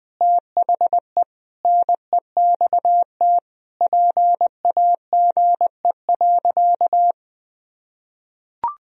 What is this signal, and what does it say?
Morse code, heard as a keyed tone, about 20 wpm, 700 Hz. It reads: THE NEXT PAGE.
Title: Sentences from Top 1000 Words - Review Part 2 - 20wpm